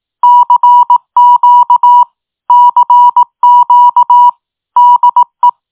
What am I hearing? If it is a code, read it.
Morse code, heard as a keyed tone, about 18 wpm, 970 Hz. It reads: CQ CQ DE
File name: tin-upgrade / QSO 2025 01 16